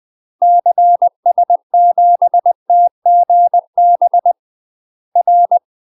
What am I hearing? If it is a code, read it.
Morse code, heard as a keyed tone, about 20 wpm, 700 Hz. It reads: CS7TGB R
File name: UK MASTER QSOs TRAINING / 20WPM / UK MASTER QSOs 0010